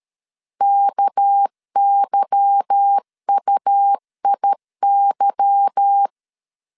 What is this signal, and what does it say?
Morse code, heard as a keyed tone, about 13 wpm, 800 Hz. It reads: KYUIY
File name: LEZIONI / 8 YJ45 / 8 Y 12wpm 2228